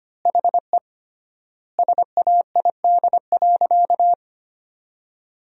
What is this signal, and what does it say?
Morse code, heard as a keyed tone, about 25 wpm, 700 Hz. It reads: HE SAID.
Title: 3x Word-Spacing - Sentences from Top 400 Words - 25wpm